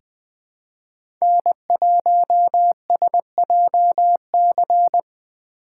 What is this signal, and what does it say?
Morse code, heard as a keyed tone, about 20 wpm, 700 Hz. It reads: N1SJC
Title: USA Call Signs - 5 Characters - 4x Character Spacing - 20wpm